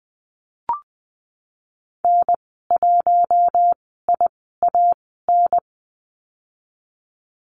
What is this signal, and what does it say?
Morse code, heard as a keyed tone, about 19 wpm, spaced as for 14 wpm, 700 Hz. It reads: N1IAN
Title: 048 Lesson 9g - Number 4 - Call Signs 20wpm